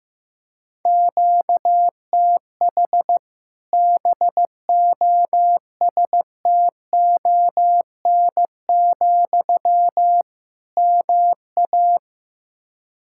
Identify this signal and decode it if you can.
Morse code, heard as a keyed tone, about 16 wpm, 700 Hz. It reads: QTH BOSTON, MA
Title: Practice QSO - Exchange 1 - Beginning - 15wpm